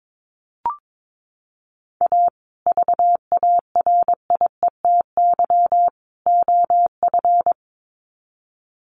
Morse code, heard as a keyed tone, about 22 wpm, 700 Hz. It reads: A VARIETY OF
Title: Sets of 3 Words 22wpm